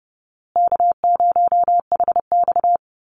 Morse code, heard as a keyed tone, about 30 wpm, 700 Hz. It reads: K0HX
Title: Easy Call Signs Repeated 3x - 30wpm